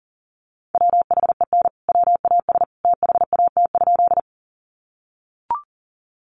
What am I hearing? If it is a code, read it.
Morse code, heard as a keyed tone, about 40 wpm, 700 Hz. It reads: WHEN WAS THAT?
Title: Sentences from Top 100 Words 40wpm